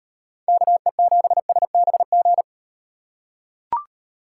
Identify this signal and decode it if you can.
Morse code, heard as a keyed tone, about 38 wpm, 700 Hz. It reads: KE7SBG